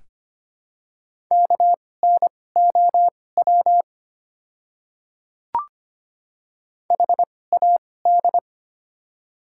Morse code, heard as a keyed tone, about 25 wpm, spaced as for 18 wpm, 690 Hz. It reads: KNOW HAD
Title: 2x Character Spacing - Top 100 Words - 25wpm